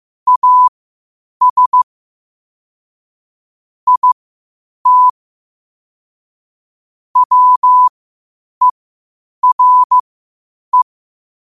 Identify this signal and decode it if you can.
Morse code, heard as a keyed tone, about 15 wpm, spaced as for 8 wpm, 1000 Hz. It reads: AS IT WERE
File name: qotdfiles / 8wpm